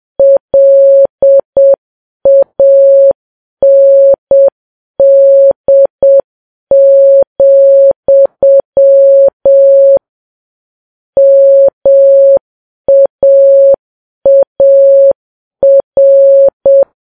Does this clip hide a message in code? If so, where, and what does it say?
Yes: Morse code from 0.2 to 16.8 s, heard as a keyed tone, about 7 wpm, 550 Hz. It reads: LAND, MAAR